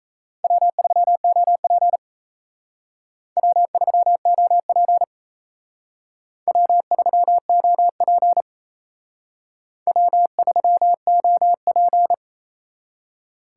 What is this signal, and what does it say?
Morse code, heard as a keyed tone, about 36 wpm, 700 Hz. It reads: W3OP W3OP W3OP W3OP